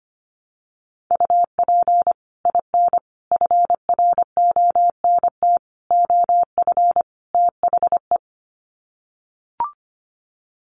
Morse code, heard as a keyed tone, about 25 wpm, 700 Hz. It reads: UP IN FRONT OF THE
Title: Sets of 5 Words Encore2 25wpm